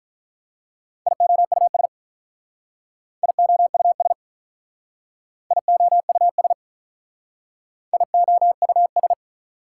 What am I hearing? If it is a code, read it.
Morse code, heard as a keyed tone, about 45 wpm, 690 Hz. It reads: IOUS IOUS IOUS IOUS